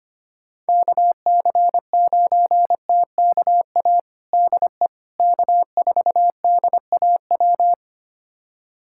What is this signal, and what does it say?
Morse code, heard as a keyed tone, about 25 wpm, 700 Hz. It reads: KC9TKA DE K4DAW
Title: USA Callsign DE Callsign 25wpm